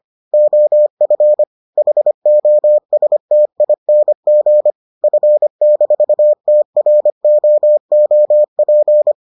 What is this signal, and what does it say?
Morse code, heard as a keyed tone, about 25 wpm, 600 Hz. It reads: OF HOSTING F-TROOP